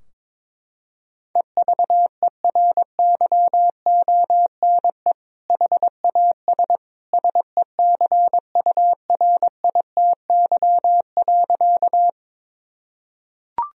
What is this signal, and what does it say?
Morse code, heard as a keyed tone, about 22 wpm, 700 Hz. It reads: EVERYONE HAS SECURITY.
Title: Sentences from Top 1000 Words - Review Part 4 - 22wpm